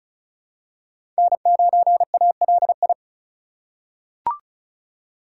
Morse code, heard as a keyed tone, about 34 wpm, 700 Hz. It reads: N9ALI
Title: USA Call Signs - 5 Characters - 4x Character Spacing - 35wpm